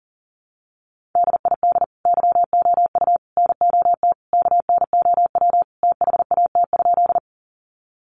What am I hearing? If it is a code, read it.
Morse code, heard as a keyed tone, about 40 wpm, 700 Hz. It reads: DID YOU NOT KNOW THAT?
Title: Sentences from Top 600 Words 40wpm